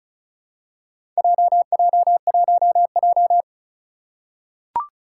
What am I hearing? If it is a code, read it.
Morse code, heard as a keyed tone, about 36 wpm, 700 Hz. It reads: JJ1J
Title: Four Challenging Letter-Number Combinations 35wpm